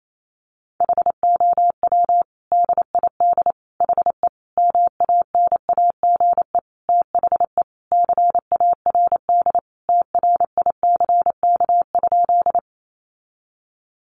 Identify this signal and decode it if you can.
Morse code, heard as a keyed tone, about 28 wpm, 700 Hz. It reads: HOW DID HE MANAGE THE CARD TRICK?
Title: Sentences from Top 1500 Words 28wpm